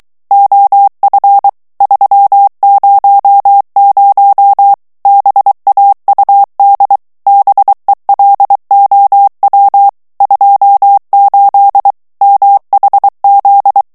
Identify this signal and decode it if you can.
Morse code, heard as a keyed tone, about 23 wpm, 800 Hz. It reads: OF 300 BAUD BELOW 28 MHZ
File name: ANGLAIS texte clair 3 vitesse 22 mots minute